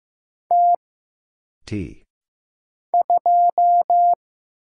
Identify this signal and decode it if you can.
Morse code, heard as a keyed tone, about 15 wpm, 700 Hz. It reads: T 2
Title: Single Letter-Number Alt - Mind-Melt 15wpm